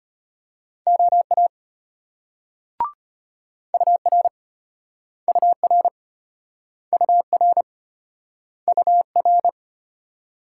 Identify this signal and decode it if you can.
Morse code, heard as a keyed tone, about 33 wpm, 700 Hz. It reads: OA UR UR UR UR